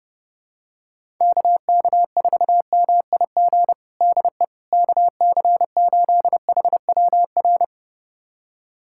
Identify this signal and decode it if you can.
Morse code, heard as a keyed tone, about 30 wpm, 700 Hz. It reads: KK4MIG DE KC8HWR
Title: USA Callsign DE Callsign 30wpm